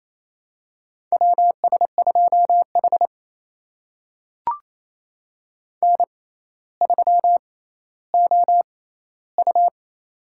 Morse code, heard as a keyed tone, about 28 wpm, spaced as for 14 wpm, 700 Hz. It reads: WS2H N3OU
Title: USA Call Signs - 6x Character Spacing - 28wpm